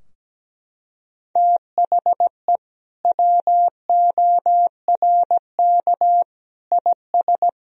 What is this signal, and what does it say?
Morse code, heard as a keyed tone, about 17 wpm, 700 Hz. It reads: THE WORK IS
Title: Sentences from Top 1000 Words - Review Part 4 - 17wpm